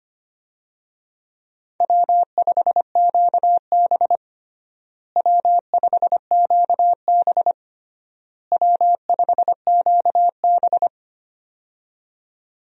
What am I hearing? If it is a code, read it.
Morse code, heard as a keyed tone, about 25 wpm, 700 Hz. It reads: W5QB W5QB W5QB
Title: USA Call Signs - 4 Characters - Repeated 3x - 25wpm